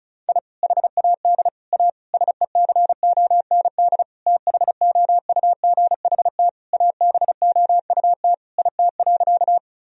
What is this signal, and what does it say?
Morse code, heard as a keyed tone, about 36 wpm, 700 Hz. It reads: I HAD A SECOND THOUGHT ABOUT IT.
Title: Sentences from Top 1000 Words - Review Part 1 - 35wpm